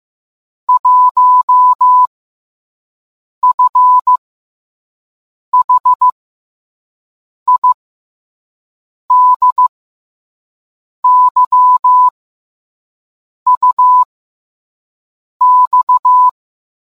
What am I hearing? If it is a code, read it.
Morse code, heard as a keyed tone, about 15 wpm, 1000 Hz. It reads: 1 F H I D Y U X